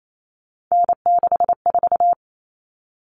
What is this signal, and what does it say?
Morse code, heard as a keyed tone, about 28 wpm, 700 Hz. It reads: N64
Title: Common Cut Numbers - 3-Digit 28wpm